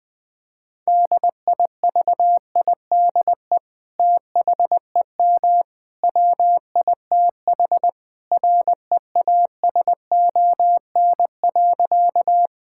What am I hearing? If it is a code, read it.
Morse code, heard as a keyed tone, about 20 wpm, 700 Hz. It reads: DIVIDE THEM WITH REASON.
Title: Sentences from Top 1600 Words 20wpm